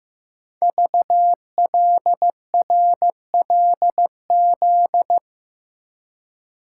Morse code, heard as a keyed tone, about 15 wpm, 700 Hz. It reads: VLRLZ
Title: Code Groups - 5 Letters - 15wpm